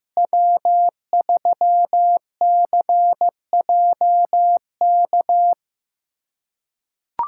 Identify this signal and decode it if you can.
Morse code, heard as a keyed tone, about 15 wpm, 700 Hz. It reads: W3CJK
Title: USA Callsign DE Callsign - 5 Characters - 15wpm